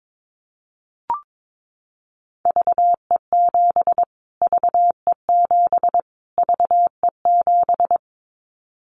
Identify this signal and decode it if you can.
Morse code, heard as a keyed tone, about 22 wpm, 700 Hz. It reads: VE7 VE7 VE7
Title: Callsign Prefixes for Top 25 DXCC - Repeated 3x - 22wpm